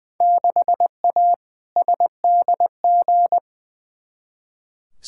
Morse code, heard as a keyed tone, about 20 wpm, 700 Hz. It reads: 6A SDG
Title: Field Day Exchanges Encore - 20wpm